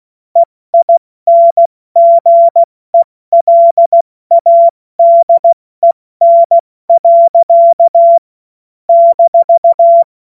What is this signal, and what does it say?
Morse code, heard as a keyed tone, about 16 wpm, spaced as for 14 wpm, 680 Hz. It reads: EINGELADEN. -
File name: Morsekurs / Witze / Witz 05 März 2024 70 F80